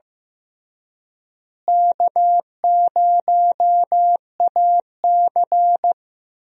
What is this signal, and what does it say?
Morse code, heard as a keyed tone, about 15 wpm, 700 Hz. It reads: K0AC
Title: USA Call Signs - 4 Characters - Repeated 2x - 15wpm